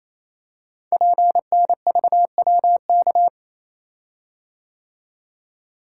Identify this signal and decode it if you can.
Morse code, heard as a keyed tone, about 27 wpm, 700 Hz. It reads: PNVWK
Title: Code Groups - 5 Letters - 28wpm